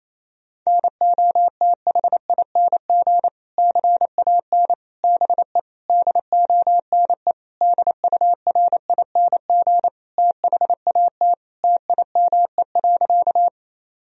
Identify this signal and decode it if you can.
Morse code, heard as a keyed tone, about 28 wpm, 700 Hz. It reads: NOTHING CAN BE DONE DURING THAT TIME.
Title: Sentences from Top 1000 Words - Review Part 2 - 28wpm